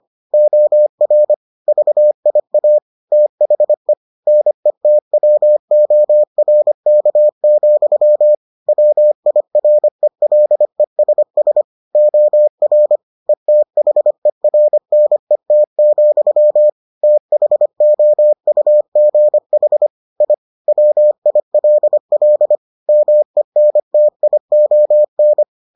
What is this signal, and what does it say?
Morse code, heard as a keyed tone, about 25 wpm, 600 Hz. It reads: OR VIA THE NETWORK, WIRELESS OR ETHERNET, THOUGH I WILL MENTION